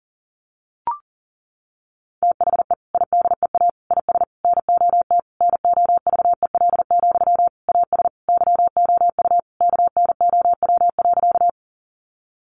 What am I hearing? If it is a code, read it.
Morse code, heard as a keyed tone, about 40 wpm, 700 Hz. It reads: THE IDEA IS NOT NOVEL, AS YOU KNOW.
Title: Sentences from Top 1300 Words 40wpm